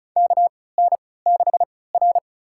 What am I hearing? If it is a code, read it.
Morse code, heard as a keyed tone, about 36 wpm, spaced as for 20 wpm, 700 Hz. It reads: KN6R
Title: USA Call Signs - 4 Characters - 3x Character Spacing - 35wpm